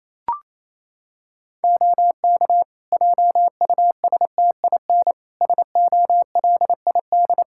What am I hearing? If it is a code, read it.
Morse code, heard as a keyed tone, about 28 wpm, 700 Hz. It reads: OK JUSTIN SOLID
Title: Practice QSO - Exchange 3 - More Info - 28wpm